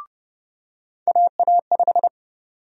Morse code, heard as a keyed tone, about 32 wpm, 700 Hz. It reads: AA5
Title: USA Call Sign Prefixes - 30wpm